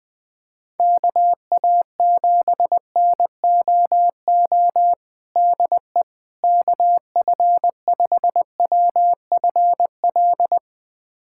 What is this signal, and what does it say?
Morse code, heard as a keyed tone, about 20 wpm, 700 Hz. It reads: KA7NOO DE KF5WFL
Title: USA Callsign DE Callsign 20wpm